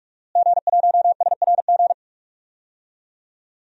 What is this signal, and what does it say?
Morse code, heard as a keyed tone, about 46 wpm, 700 Hz. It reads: G1SRG
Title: DXCC-100 Call Signs - 5 Characters - Repeated 2x - 45wpm